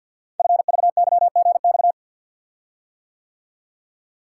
Morse code, heard as a keyed tone, about 52 wpm, 690 Hz. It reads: FVYGX
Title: Code Groups - 5 Letters - 50wpm